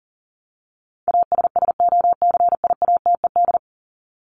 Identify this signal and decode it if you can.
Morse code, heard as a keyed tone, about 40 wpm, 700 Hz. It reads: ASSOCIATED
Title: ED Suffix - Top 100 Words with Base Words 40wpm